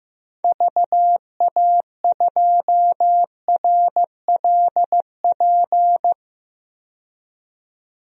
Morse code, heard as a keyed tone, about 15 wpm, 690 Hz. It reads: VA2RLP
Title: DXCC-100 Call Signs - 6 Characters - 15wpm